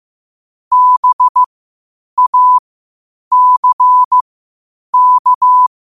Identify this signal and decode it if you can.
Morse code, heard as a keyed tone, about 15 wpm, spaced as for 8 wpm, 1000 Hz. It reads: BACK
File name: qotdfiles / 8wpm